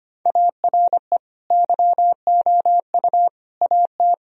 Morse code, heard as a keyed tone, about 25 wpm, 700 Hz. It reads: ARE YOU AT
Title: Sentences from Top 1000 Words - Review Part 4 - 25wpm